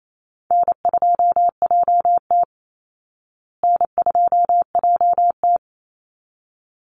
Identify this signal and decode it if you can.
Morse code, heard as a keyed tone, about 28 wpm, 700 Hz. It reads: N2JT N2JT
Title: Easy Call Signs Repeated 2x - 28wpm